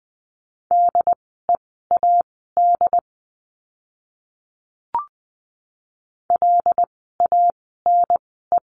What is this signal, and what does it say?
Morse code, heard as a keyed tone, about 20 wpm, spaced as for 15 wpm, 700 Hz. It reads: DEAD LANE
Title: 075 Lesson 13f - Letter L - Words 20wpm